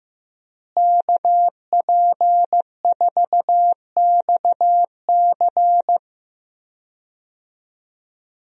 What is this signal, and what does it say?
Morse code, heard as a keyed tone, about 16 wpm, 700 Hz. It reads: KP4XC
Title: DXCC-100 Call Signs - 5 Characters - Repeated 3x - 15wpm